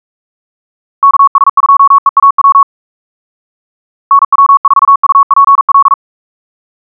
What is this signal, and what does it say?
Morse code, heard as a keyed tone, about 42 wpm, 1110 Hz. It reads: KS2EAW NW4UPL
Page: SampleHST1.wav <-- 158cpm (=47wpm(paris)) 1111Hz sinusoidal waveform